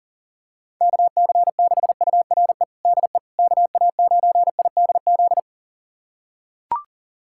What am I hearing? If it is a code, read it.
Morse code, heard as a keyed tone, about 40 wpm, 700 Hz. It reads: KC6URE DE KA9IDZ
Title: USA Callsign DE Callsign - 6 Characters - 40wpm